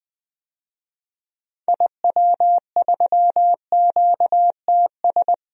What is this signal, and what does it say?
Morse code, heard as a keyed tone, about 20 wpm, 700 Hz. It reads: IW3QTS
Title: DXCC-100 Call Signs - 6 Characters - 20wpm